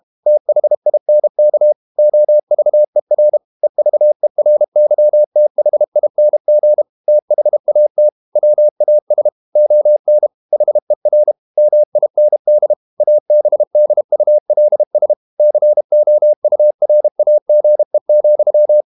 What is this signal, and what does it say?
Morse code, heard as a keyed tone, about 32 wpm, 600 Hz. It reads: THINK OVER EVERYTHING THAT WAS ON HER MIND ABDULS COURAGE,
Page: Never 31wpm | CW med Gnister